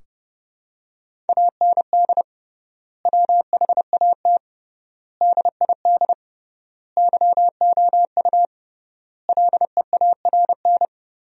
Morse code, heard as a keyed tone, about 32 wpm, 700 Hz. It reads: AND WHAT DID YOU LEARN
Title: 3x Word-Spacing - Sentences from Top 300 Words - 30wpm